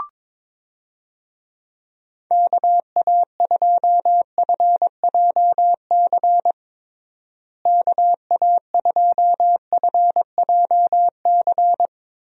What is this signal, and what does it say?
Morse code, heard as a keyed tone, about 21 wpm, 700 Hz. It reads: KA2FJC KA2FJC
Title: USA Call Signs - 6 Characters - Repeated 2x - 22wpm